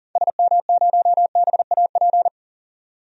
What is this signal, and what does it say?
Morse code, heard as a keyed tone, about 37 wpm, 700 Hz. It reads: SM0BAP
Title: DXCC-100 Call Signs - 6 Characters - 40wpm